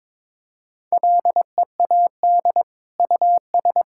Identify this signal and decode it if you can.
Morse code, heard as a keyed tone, about 21 wpm, 700 Hz. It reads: LEAD US
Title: Sentences from Top 1600 Words 22wpm